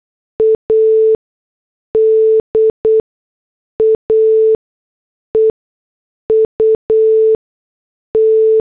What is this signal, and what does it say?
Morse code, heard as a keyed tone, about 8 wpm, spaced as for 6 wpm, 440 Hz. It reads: ADA EUT